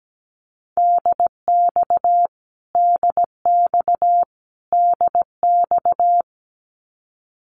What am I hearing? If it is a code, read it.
Morse code, heard as a keyed tone, about 17 wpm, 700 Hz. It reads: DX DX DX
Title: Abbreviations Repeated 3x - 17wpm